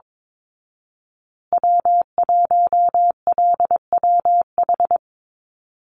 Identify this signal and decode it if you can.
Morse code, heard as a keyed tone, about 22 wpm, 700 Hz. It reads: W1LWH
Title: Call Signs Repeated 2x - 22wpm